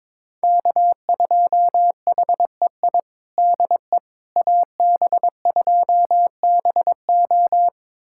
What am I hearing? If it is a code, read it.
Morse code, heard as a keyed tone, about 23 wpm, 700 Hz. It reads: K2HEI DE AB2BO
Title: USA Callsign DE Callsign - 5 Characters - 22wpm